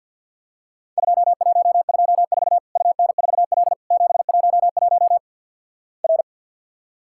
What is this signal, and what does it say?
Morse code, heard as a keyed tone, about 50 wpm, 700 Hz. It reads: 2124 UN4L 711 R
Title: Worked All Europe DX Contest - QTC Exchange with Repeats - 50wpm